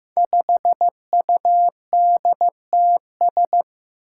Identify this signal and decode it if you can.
Morse code, heard as a keyed tone, about 15 wpm, 700 Hz. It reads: 5UDTS
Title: Five Challenging Letter-Number Combinations 15wpm